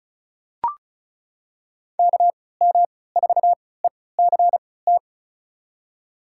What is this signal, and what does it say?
Morse code, heard as a keyed tone, about 36 wpm, spaced as for 20 wpm, 700 Hz. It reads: KM4ECT
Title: USA Call Signs - 3x Character Spacing - 35wpm